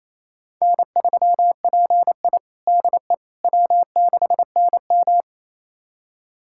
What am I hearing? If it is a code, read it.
Morse code, heard as a keyed tone, about 27 wpm, 700 Hz. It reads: N3PI DE W6NM
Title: USA Callsign DE Callsign - 4 Characters - 28wpm